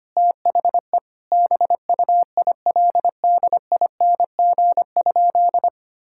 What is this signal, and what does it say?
Morse code, heard as a keyed tone, about 25 wpm, 700 Hz. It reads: THE BUILDING?
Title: Sentences from Top 1000 Words - Review Part 6 - 25wpm